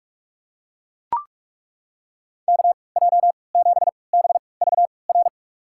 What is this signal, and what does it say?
Morse code, heard as a keyed tone, about 41 wpm, spaced as for 25 wpm, 700 Hz. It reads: KJ7BVR
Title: USA Call Signs - 6 Characters - 3x Character Spacing - 45wpm